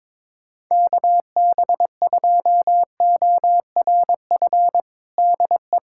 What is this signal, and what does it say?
Morse code, heard as a keyed tone, about 22 wpm, 700 Hz. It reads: KB2ORF DE
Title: USA Callsign DE Callsign - 6 Characters - 22wpm